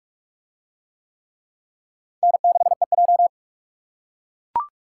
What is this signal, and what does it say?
Morse code, heard as a keyed tone, about 44 wpm, 700 Hz. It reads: N6EJ